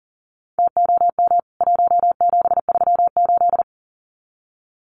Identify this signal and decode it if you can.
Morse code, heard as a keyed tone, about 40 wpm, 700 Hz. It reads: TOM 1738
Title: CWT Exchange 40wpm